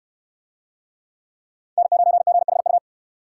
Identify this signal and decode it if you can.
Morse code, heard as a keyed tone, about 69 wpm, 700 Hz. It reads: N9GHU